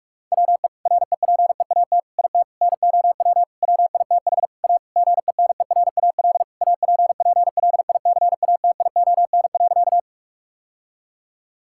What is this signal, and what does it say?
Morse code, heard as a keyed tone, about 46 wpm, 700 Hz. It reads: WE REPEAT IT NOW WITH A GENERAL APPLICATION.
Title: Sentences from Top 1600 Words 45wpm